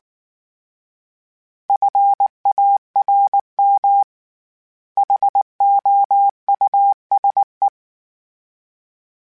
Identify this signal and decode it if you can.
Morse code, heard as a keyed tone, about 19 wpm, 800 Hz. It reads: FARM HOUSE
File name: General / General-mp3s / Houses @18